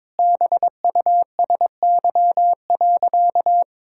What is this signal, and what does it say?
Morse code, heard as a keyed tone, about 22 wpm, 700 Hz. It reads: BUSY.
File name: Sentences with 7-Word Phrases - 22wpm